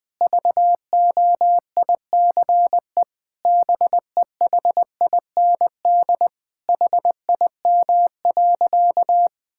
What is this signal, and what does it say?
Morse code, heard as a keyed tone, about 19 wpm, 700 Hz. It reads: VOICE BEHIND HIM.
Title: Sentences from Top 1600 Words 20wpm